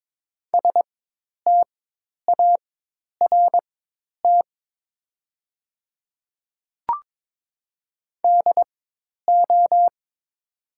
Morse code, heard as a keyed tone, about 22 wpm, spaced as for 10 wpm, 690 Hz. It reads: START DO
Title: 4x Character Spacing - Top 500 Words - 22wpm